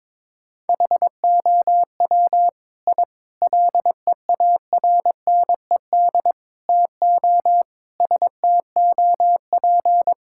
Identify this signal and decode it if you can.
Morse code, heard as a keyed tone, about 22 wpm, 700 Hz. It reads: HOW I LEARNED TO STOP
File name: practice / 22wpm / Sets of 8 Words - 22wpm